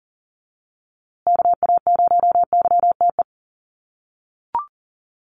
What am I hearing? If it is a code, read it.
Morse code, heard as a keyed tone, about 40 wpm, 700 Hz. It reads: KA0YTE
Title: USA Call Signs Repeated 2x 40wpm